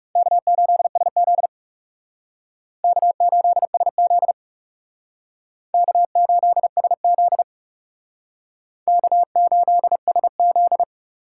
Morse code, heard as a keyed tone, about 38 wpm, 700 Hz. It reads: K8SZ K8SZ K8SZ K8SZ